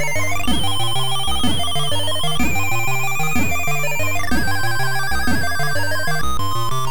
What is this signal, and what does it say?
There is no code.